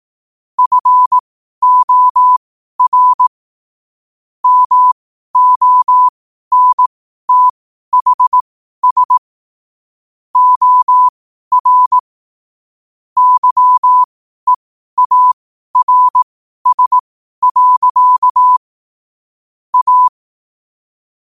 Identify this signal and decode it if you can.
Morse code, heard as a keyed tone, about 18 wpm, spaced as for 12 wpm, 1000 Hz. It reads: FOR MONTHS OR YEARS. A
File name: qotdfiles / 12wpm